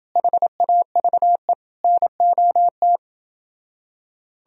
Morse code, heard as a keyed tone, about 29 wpm, 700 Hz. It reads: HAVE NOT